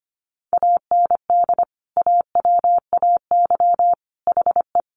Morse code, heard as a keyed tone, about 25 wpm, 700 Hz. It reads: AND AWAY HE
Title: Sentences from Top 600 Words 25wpm